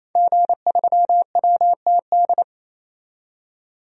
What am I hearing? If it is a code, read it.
Morse code, heard as a keyed tone, about 28 wpm, 700 Hz. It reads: G3WTD
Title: DXCC-100 Call Signs - 5 Characters - 28wpm